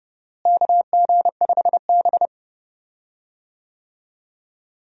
Morse code, heard as a keyed tone, about 31 wpm, 700 Hz. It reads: KG5B
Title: USA Call Signs - 4 Characters - Repeated 2x - 30wpm